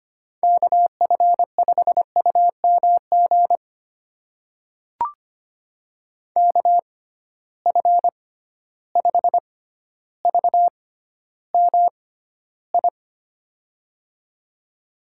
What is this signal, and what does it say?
Morse code, heard as a keyed tone, about 25 wpm, spaced as for 13 wpm, 700 Hz. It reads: KF5UMG KF5VMI
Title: USA Call Signs - 6 Characters - 6x Character Spacing - 25wpm